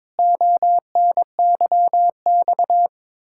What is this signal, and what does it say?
Morse code, heard as a keyed tone, about 21 wpm, 700 Hz. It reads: ONYX
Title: Pangram Challenge v2 - 22wpm